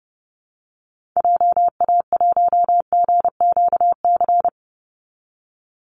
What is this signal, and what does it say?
Morse code, heard as a keyed tone, about 29 wpm, 700 Hz. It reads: JA1GQC
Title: Call Signs - 30wpm